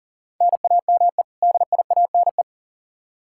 Morse code, heard as a keyed tone, about 40 wpm, 700 Hz. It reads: NAME DIANE
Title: K1USN SST QSO Practice 40wpm